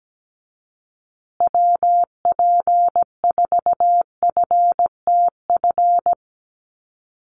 Morse code, heard as a keyed tone, about 17 wpm, 700 Hz. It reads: WP4FTF
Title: USA Call Signs Repeated 3x 17wpm